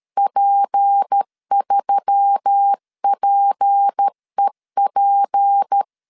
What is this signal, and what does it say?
Morse code, heard as a keyed tone, about 13 wpm, 800 Hz. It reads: P3PEP
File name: LEZIONI / 6 PKX0 / 6 P 12wpm 2033